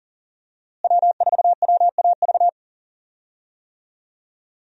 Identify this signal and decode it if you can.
Morse code, heard as a keyed tone, about 40 wpm, 700 Hz. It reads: W4WAV
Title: USA Call Signs - 5 Characters - Repeated 2x - 40wpm